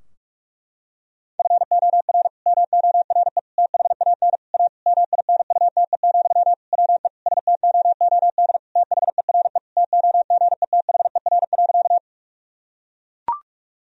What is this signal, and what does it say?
Morse code, heard as a keyed tone, about 42 wpm, 700 Hz. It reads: FOR MORE THAN A MINUTE, WE STOOD THERE TOGETHER.
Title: Sentences from Top 1000 Words - Review Part 2 - 45wpm